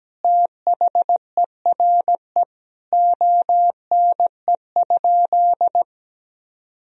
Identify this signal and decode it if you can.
Morse code, heard as a keyed tone, about 17 wpm, 700 Hz. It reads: THERE ONE?
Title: Sentences from Top 100 Words and Abbreviations 17wpm